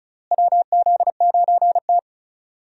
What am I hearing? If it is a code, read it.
Morse code, heard as a keyed tone, about 36 wpm, 700 Hz. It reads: WZ9T
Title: USA Call Signs - 4 Characters - Repeated 2x - 35wpm